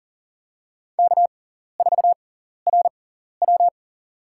Morse code, heard as a keyed tone, about 40 wpm, spaced as for 14 wpm, 700 Hz. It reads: K4RW
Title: USA Call Signs - 6x Character Spacing - 40wpm